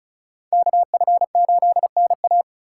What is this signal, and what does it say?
Morse code, heard as a keyed tone, about 41 wpm, 700 Hz. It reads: KF8NA
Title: Speed Racing - USA 2x2 Call Signs 35wpm